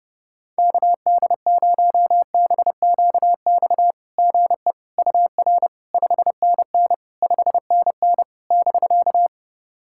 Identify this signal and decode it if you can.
Morse code, heard as a keyed tone, about 32 wpm, 700 Hz. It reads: KD0BQX GE UR 5NN 5NN <BK>
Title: SOTA QSO Practice 30wpm